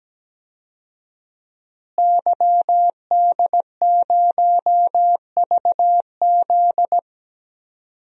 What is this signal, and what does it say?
Morse code, heard as a keyed tone, about 17 wpm, 700 Hz. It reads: YD0VZ
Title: DXCC-100 Call Signs - 5 Characters - Repeated 2x - 17wpm